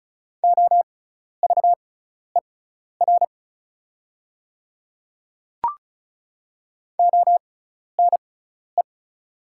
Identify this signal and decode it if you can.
Morse code, heard as a keyed tone, about 36 wpm, spaced as for 12 wpm, 700 Hz. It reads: OVER ONE
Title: 6x Character Spacing - Top 200 Words - 35wpm